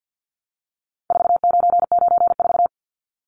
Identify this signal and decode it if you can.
Morse code, heard as a keyed tone, about 48 wpm, 700 Hz. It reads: 4994
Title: Numbers - 4-Digit 50wpm